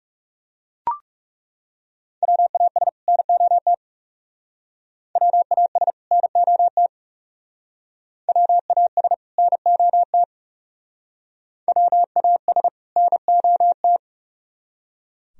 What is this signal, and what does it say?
Morse code, heard as a keyed tone, about 38 wpm, 700 Hz. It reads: WAS NOT WAS NOT WAS NOT WAS NOT